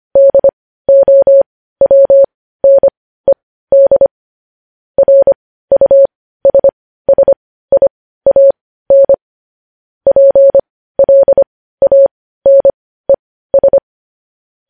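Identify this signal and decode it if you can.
Morse code, heard as a keyed tone, about 25 wpm, spaced as for 15 wpm, 550 Hz. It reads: DOWNED RUSSIAN PLANES
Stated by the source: Morse code news headlines sent at 15, preceded by vvv vvv.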